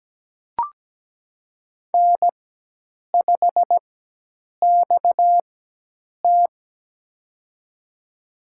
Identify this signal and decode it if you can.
Morse code, heard as a keyed tone, about 17 wpm, spaced as for 8 wpm, 700 Hz. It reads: N5XT
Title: USA Call Signs - 4 Characters - 4x Character Spacing - 17wpm